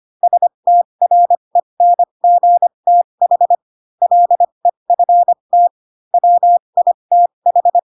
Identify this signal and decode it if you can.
Morse code, heard as a keyed tone, about 25 wpm, spaced as for 22 wpm, 700 Hz. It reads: STRENGTH LEFT WITH
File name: Pinocchio / Pinocchio ch2 18 25 ik1hge